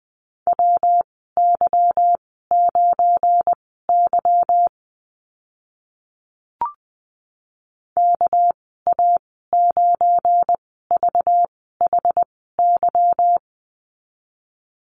Morse code, heard as a keyed tone, about 20 wpm, spaced as for 14 wpm, 700 Hz. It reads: WY9Y KA9VHY
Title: 213 Lesson 34g - Number 8 - Call Signs 20wpm